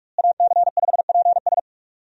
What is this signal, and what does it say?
Morse code, heard as a keyed tone, about 46 wpm, 700 Hz. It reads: AC5PS
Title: USA Call Signs - 5 Characters - Repeated 3x - 45wpm